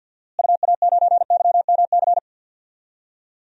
Morse code, heard as a keyed tone, about 49 wpm, 700 Hz. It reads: UA9YMC
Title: DXCC-100 Call Signs - 6 Characters - Repeated 2x - 50wpm